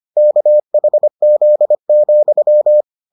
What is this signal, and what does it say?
Morse code, heard as a keyed tone, about 25 wpm, 600 Hz. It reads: KHZ,